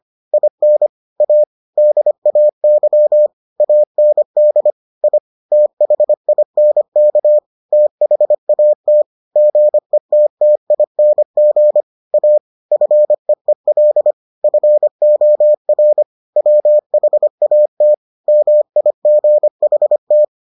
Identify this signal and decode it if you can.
Morse code, heard as a keyed tone, about 25 wpm, 600 Hz. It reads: IN A DAY AND I THINK THAT GETTING A FEEL FOR WHAT MIGHT